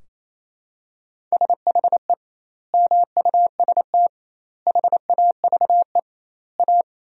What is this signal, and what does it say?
Morse code, heard as a keyed tone, about 28 wpm, 700 Hz. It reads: SHE MUST HAVE A
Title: 2x Word-Spacing - Sentences from Top 300 Words - 28wpm